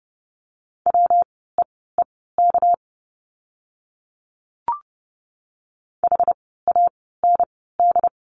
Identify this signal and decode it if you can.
Morse code, heard as a keyed tone, about 30 wpm, spaced as for 17 wpm, 700 Hz. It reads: WEEK HAND
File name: course / 30:10wpm / 230 Lesson 37f - Letter x - Words 30wpm